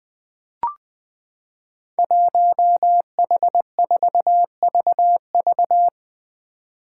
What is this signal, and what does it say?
Morse code, heard as a keyed tone, about 20 wpm, 700 Hz. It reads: 1H4VV
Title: Five Challenging Letter-Number Combinations 20wpm